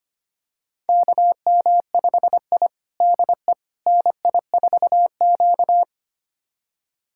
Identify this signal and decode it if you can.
Morse code, heard as a keyed tone, about 25 wpm, 700 Hz. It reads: KM5I DE NI4Q
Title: USA Callsign DE Callsign - 4 Characters - 25wpm